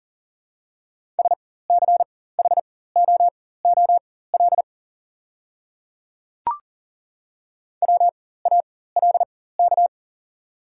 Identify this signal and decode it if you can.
Morse code, heard as a keyed tone, about 40 wpm, spaced as for 19 wpm, 700 Hz. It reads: SCHOOL WALK
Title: 4x Character Spacing - Top 500 Words - 40wpm